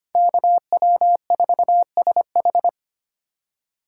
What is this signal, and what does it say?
Morse code, heard as a keyed tone, about 31 wpm, 700 Hz. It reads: KW4SH
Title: Speed Racing - USA 2x2 Call Signs 25wpm